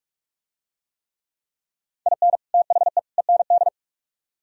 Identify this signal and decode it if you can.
Morse code, heard as a keyed tone, about 44 wpm, 700 Hz. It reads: IN THE END